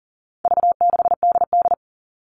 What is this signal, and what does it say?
Morse code, heard as a keyed tone, about 40 wpm, 700 Hz. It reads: V6DD
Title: All Cut Numbers - 4-Digit 40wpm